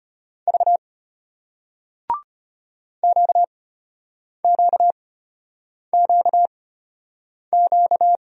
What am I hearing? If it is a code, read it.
Morse code, heard as a keyed tone, about 34 wpm, spaced as for 8 wpm, 700 Hz. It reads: V QQQQ